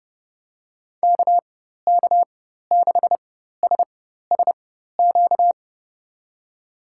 Morse code, heard as a keyed tone, about 32 wpm, spaced as for 14 wpm, 700 Hz. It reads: KK6SSQ
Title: USA Call Signs - 4x Character Spacing - 30wpm